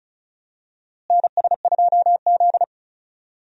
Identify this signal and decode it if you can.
Morse code, heard as a keyed tone, about 34 wpm, 700 Hz. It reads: NS2Z
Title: USA Call Signs - 4 Characters - Repeated 2x - 35wpm